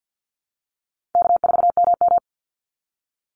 Morse code, heard as a keyed tone, about 50 wpm, 700 Hz. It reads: K4MM
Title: Easy Call Signs - 50wpm